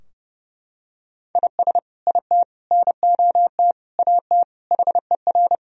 Morse code, heard as a keyed tone, about 32 wpm, 700 Hz. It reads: IS IT NOT AT HER
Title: Sentences from Top 1000 Words - Review Part 4 - 30wpm